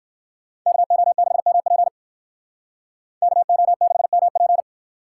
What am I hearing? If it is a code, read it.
Morse code, heard as a keyed tone, about 56 wpm, 700 Hz. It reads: KO6MP KO6MP
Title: Speed Racing - USA 2x2 Call Signs 45wpm